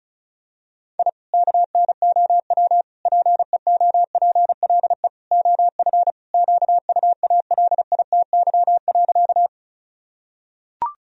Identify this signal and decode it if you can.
Morse code, heard as a keyed tone, about 36 wpm, 700 Hz. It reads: I KNOW PEOPLE OF QUALITY.
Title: Sentences from Top 1000 Words - Review Part 4 - 35wpm